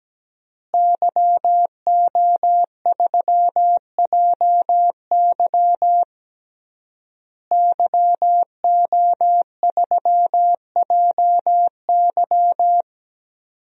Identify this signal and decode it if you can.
Morse code, heard as a keyed tone, about 17 wpm, 700 Hz. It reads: YO3JY YO3JY
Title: DXCC-100 Call Signs - 5 Characters - Repeated 3x - 17wpm